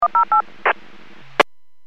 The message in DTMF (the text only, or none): 108